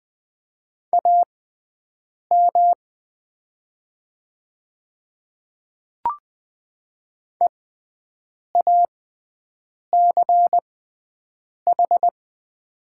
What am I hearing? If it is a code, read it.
Morse code, heard as a keyed tone, about 20 wpm, spaced as for 7 wpm, 700 Hz. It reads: AM EACH
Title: 6x Character Spacing - Top 500 Words - 20wpm